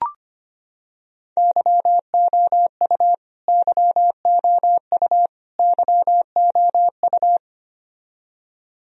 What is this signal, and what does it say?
Morse code, heard as a keyed tone, about 25 wpm, 700 Hz. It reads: YOU YOU YOU
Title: Top 100 Words Repeated 25wpm